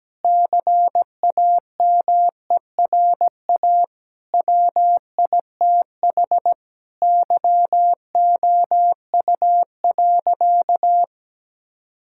Morse code, heard as a keyed tone, about 17 wpm, 700 Hz. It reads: CAMERA WITH YOU.
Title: Sentences from Top 1000 Words - Review Part 5 - 17wpm